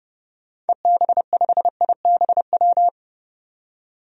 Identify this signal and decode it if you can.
Morse code, heard as a keyed tone, about 30 wpm, 700 Hz. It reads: EB5IBW
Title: DXCC-100 Call Signs - 6 Characters - 30wpm